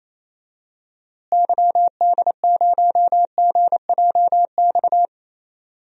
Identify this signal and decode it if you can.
Morse code, heard as a keyed tone, about 28 wpm, 700 Hz. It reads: YD0GJX
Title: DXCC-100 Call Signs - 6 Characters - Repeated 2x - 28wpm